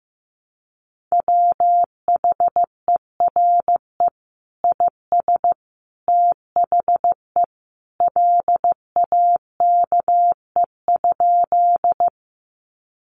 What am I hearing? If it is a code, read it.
Morse code, heard as a keyed tone, about 15 wpm, 700 Hz. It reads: WHERE IS THE LAKE?
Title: Sentences from Top 1100 Words 15wpm